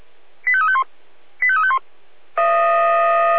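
Five-tone selective call (ZVEI), heard as quick runs of five tones, tones of about 70 ms. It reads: 85351 85351
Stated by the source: ZVEI five-tone selective call received via SDR.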